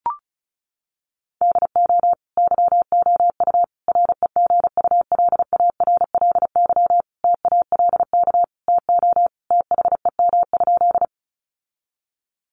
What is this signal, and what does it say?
Morse code, heard as a keyed tone, about 34 wpm, 700 Hz. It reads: DO YOU REGULARLY TALK TO THEM?
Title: Sentences from Top 1400 Words 35wpm